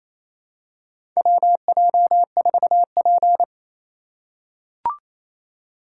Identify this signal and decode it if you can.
Morse code, heard as a keyed tone, about 28 wpm, 700 Hz. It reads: WJ4P
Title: USA Call Signs - 4 Characters - 4x Character Spacing - 28wpm